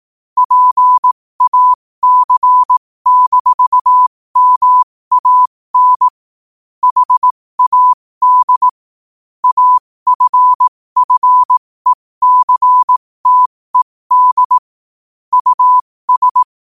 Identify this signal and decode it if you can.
Morse code, heard as a keyed tone, about 18 wpm, spaced as for 15 wpm, 1000 Hz. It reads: PAC-MAN HAD AFFECTED US
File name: qotdfiles / 15wpm